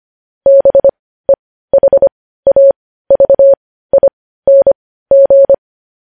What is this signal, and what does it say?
Morse code, heard as a keyed tone, about 25 wpm, spaced as for 15 wpm, 550 Hz. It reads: BEHAVING
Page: News headlines at 15